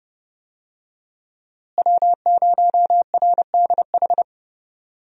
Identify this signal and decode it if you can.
Morse code, heard as a keyed tone, about 31 wpm, 700 Hz. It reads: W0RDH
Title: USA Call Signs - 5 Characters - Repeated 2x - 30wpm